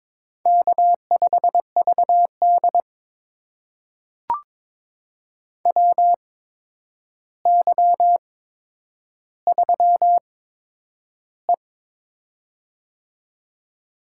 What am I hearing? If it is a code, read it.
Morse code, heard as a keyed tone, about 22 wpm, spaced as for 9 wpm, 700 Hz. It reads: K5VD WY3E
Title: USA Call Signs - 4 Characters - 8x Character Spacing - 22wpm